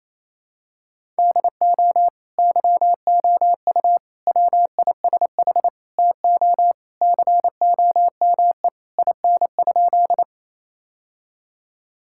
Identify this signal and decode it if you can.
Morse code, heard as a keyed tone, about 27 wpm, 700 Hz. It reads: DO YOU WISH TO COME IN?
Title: Sentences from Top 1600 Words 28wpm